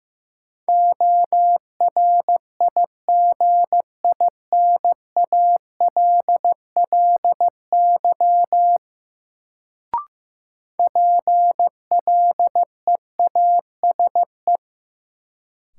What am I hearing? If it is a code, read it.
Morse code, heard as a keyed tone, about 15 wpm, 700 Hz. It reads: ORIGINALLY PLEASE
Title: Sentences from Top 1500 Words - New Word Drill - 15wpm